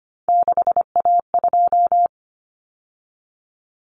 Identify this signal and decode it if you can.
Morse code, heard as a keyed tone, about 25 wpm, 700 Hz. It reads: 6A2
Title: Common Cut Numbers - 3-Digit 25wpm